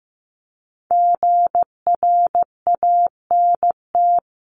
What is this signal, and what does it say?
Morse code, heard as a keyed tone, about 15 wpm, 700 Hz. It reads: GRANT
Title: CWT Exchange 15wpm